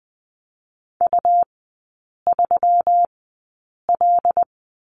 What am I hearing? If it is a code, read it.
Morse code, heard as a keyed tone, about 20 wpm, spaced as for 8 wpm, 700 Hz. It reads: U3L
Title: 121 Lesson 20e - Number 3 - Three Characters 20wpm